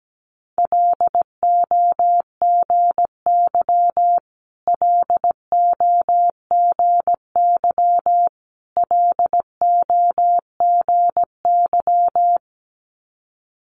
Morse code, heard as a keyed tone, about 17 wpm, 700 Hz. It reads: LOGY LOGY LOGY
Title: Multi-letter Phonemes Repeated 3x 17wpm